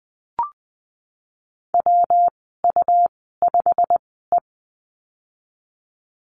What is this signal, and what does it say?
Morse code, heard as a keyed tone, about 20 wpm, spaced as for 15 wpm, 700 Hz. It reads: WU5E
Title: 122 Lesson 20g - Number 3 - Call Signs 20wpm